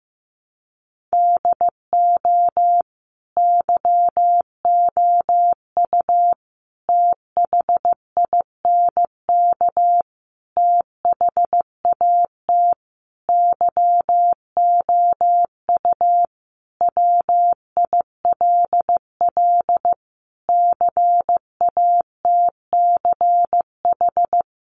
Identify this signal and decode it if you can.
Morse code, heard as a keyed tone, about 15 wpm, 700 Hz. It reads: DO YOU THINK THAT YOU WILL CATCH
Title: Sentences from Top 1500 Words 15wpm